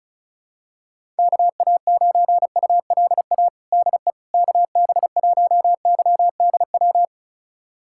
Morse code, heard as a keyed tone, about 34 wpm, 700 Hz. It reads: KA9ULA DE KB1YDW
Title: USA Callsign DE Callsign - 6 Characters - 35wpm